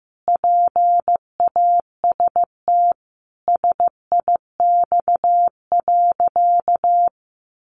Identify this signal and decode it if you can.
Morse code, heard as a keyed tone, about 14 wpm, 700 Hz. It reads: PAST SIX.
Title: Sentences from Top 600 Words 15wpm